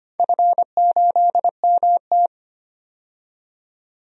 Morse code, heard as a keyed tone, about 25 wpm, 700 Hz. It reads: F8MT
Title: DXCC-100 Call Signs - 4 Characters - 25wpm